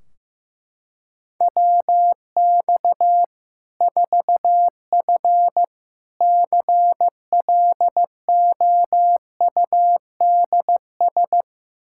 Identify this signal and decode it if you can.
Morse code, heard as a keyed tone, about 16 wpm, 700 Hz. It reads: WX 4F CLOUDS
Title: Practice QSO - Exchange 2 - Info - 15wpm